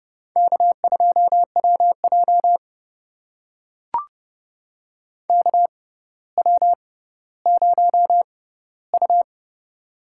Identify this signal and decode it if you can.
Morse code, heard as a keyed tone, about 30 wpm, spaced as for 15 wpm, 700 Hz. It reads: K2WJ KW0U
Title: USA Call Signs - 4 Characters - 6x Character Spacing - 30wpm